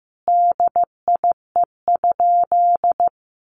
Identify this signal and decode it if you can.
Morse code, heard as a keyed tone, about 15 wpm, 700 Hz. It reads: DIE?
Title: Sentences from Top 1300 Words 15wpm